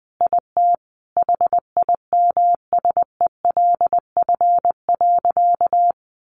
Morse code, heard as a keyed tone, about 20 wpm, 700 Hz. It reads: IT HIMSELF.
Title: Additional Easy Phrases 20wpm